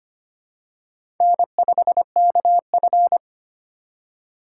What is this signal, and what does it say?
Morse code, heard as a keyed tone, about 25 wpm, 700 Hz. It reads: N5KF
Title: USA Call Signs - 4 Characters - 2x Character Spacing - 25wpm